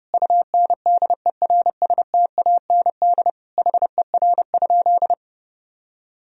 Morse code, heard as a keyed tone, about 32 wpm, 700 Hz. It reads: UNDERSTAND HER?
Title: Sentences from Top 1000 Words - Review Part 2 - 30wpm